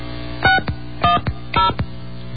DTMF, heard as telephone touch tones, tones of about 140 ms. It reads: 610
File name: mp3files / off the hook